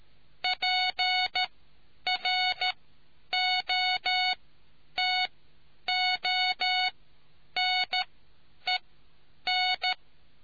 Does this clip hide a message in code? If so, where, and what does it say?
Yes: Morse code from 0.4 to 9.9 s, heard as a keyed tone, about 13 wpm, spaced as for 9 wpm, 740 Hz. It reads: PROTONEN